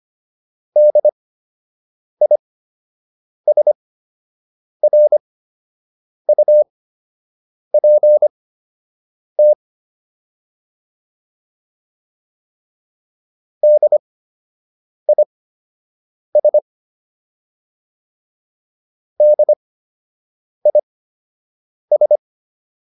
Morse code, heard as a keyed tone, about 25 wpm, spaced as for 7 wpm, 600 Hz. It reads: DISRUPT DIS DIS
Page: fundamental-practice-files